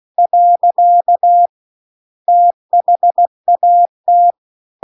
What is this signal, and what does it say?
Morse code, heard as a keyed tone, about 16 wpm, 700 Hz. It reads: . THAT